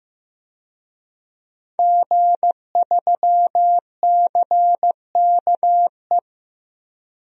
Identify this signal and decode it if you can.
Morse code, heard as a keyed tone, about 15 wpm, 700 Hz. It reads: G3CKE
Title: DXCC-100 Call Signs - 5 Characters - Repeated 2x - 15wpm